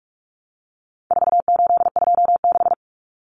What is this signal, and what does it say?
Morse code, heard as a keyed tone, about 44 wpm, 700 Hz. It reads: 4826
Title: Numbers - 4-Digit 45wpm